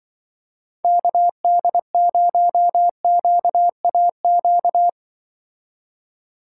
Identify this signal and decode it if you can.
Morse code, heard as a keyed tone, about 24 wpm, 700 Hz. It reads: KD0QAQ
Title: Speed Racing - USA 2x3 Call Signs 20wpm